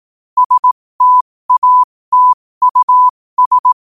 Morse code, heard as a keyed tone, about 18 wpm, spaced as for 15 wpm, 1000 Hz. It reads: STATUS
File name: qotdfiles / 15wpm